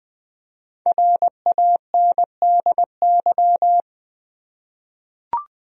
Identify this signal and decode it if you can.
Morse code, heard as a keyed tone, about 21 wpm, 700 Hz. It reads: RANDY
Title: USA Common Names - 20wpm